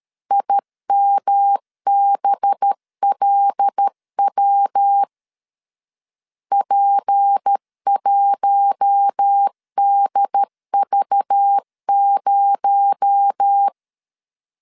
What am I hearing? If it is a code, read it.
Morse code, heard as a keyed tone, about 13 wpm, 800 Hz. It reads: IMBLW P1DV0
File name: LEZIONI / 7 GF9Z / 7 G 12wpm 2055